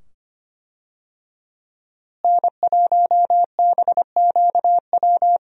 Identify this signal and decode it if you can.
Morse code, heard as a keyed tone, about 25 wpm, 700 Hz. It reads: N1BQW
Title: USA Call Signs - 5 Characters - Repeated 2x - 25wpm